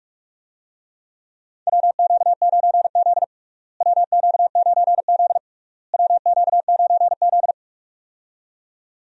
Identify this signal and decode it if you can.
Morse code, heard as a keyed tone, about 44 wpm, 700 Hz. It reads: WQ9Z WQ9Z WQ9Z
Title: USA Call Signs - 4 Characters - Repeated 3x - 45wpm